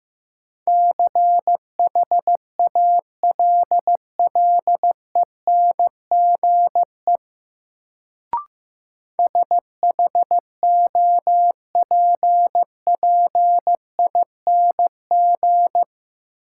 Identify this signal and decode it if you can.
Morse code, heard as a keyed tone, about 15 wpm, 700 Hz. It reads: CHALLENGE SHOPPING
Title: Sentences from Top 1000 Words - Review Part 7 - New Word Drill - 15wpm